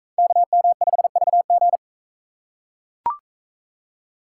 Morse code, heard as a keyed tone, about 42 wpm, 700 Hz. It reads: KM5VG